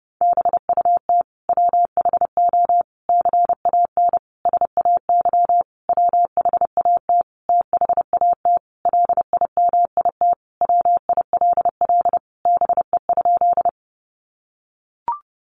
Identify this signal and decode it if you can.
Morse code, heard as a keyed tone, about 28 wpm, 700 Hz. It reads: BUT WHO CAN SAY WHAT THAT LIMIT WILL BE?
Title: Sentences from Top 1500 Words 30wpm